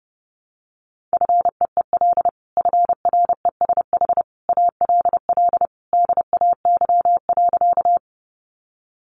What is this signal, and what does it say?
Morse code, heard as a keyed tone, about 30 wpm, 700 Hz. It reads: FEEL FRESH ALL DAY.
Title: Sentences from Top 1200 Words 30wpm